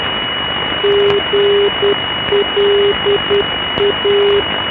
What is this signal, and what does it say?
Morse code, heard as a keyed tone, about 10 wpm, 400 Hz, with noise. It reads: GLA
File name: NDB Gland GLA